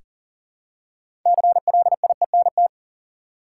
Morse code, heard as a keyed tone, about 40 wpm, 700 Hz. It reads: CLIENT
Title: Six Letter Words - 40wpm